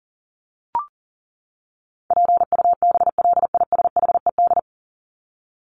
Morse code, heard as a keyed tone, about 40 wpm, 700 Hz. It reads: PUBLISHED
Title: ED Suffix - Top 100 Words with Base Words 40wpm